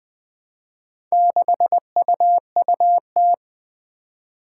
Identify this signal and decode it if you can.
Morse code, heard as a keyed tone, about 20 wpm, 700 Hz. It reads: 6UUT
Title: Four Challenging Letter-Number Combinations 20wpm